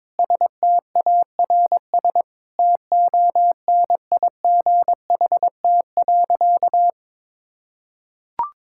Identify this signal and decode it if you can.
Morse code, heard as a keyed tone, about 22 wpm, 700 Hz. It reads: STARS TONIGHT.
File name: practice / 22wpm / Sentences with 8-Word Phrases - 22wpm